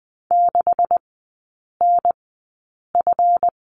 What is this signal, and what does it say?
Morse code, heard as a keyed tone, about 20 wpm, spaced as for 8 wpm, 700 Hz. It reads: 6NF
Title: 138 Lesson 23e - Letter f - Three Characters 20wpm